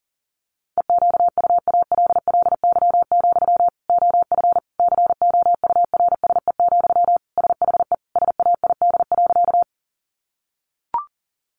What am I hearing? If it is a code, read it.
Morse code, heard as a keyed tone, about 40 wpm, 700 Hz. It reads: EQUALLY, OF COURSE, SHE SAID.
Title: Sentences from Top 1400 Words 40wpm